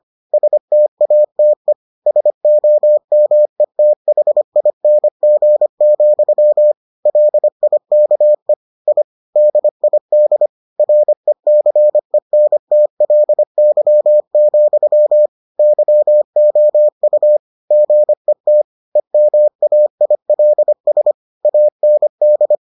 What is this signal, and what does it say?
Morse code, heard as a keyed tone, about 25 wpm, 600 Hz. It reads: STATE SOMETHING, LIKE I DID RECENTLY, YOU GET EMAILS AND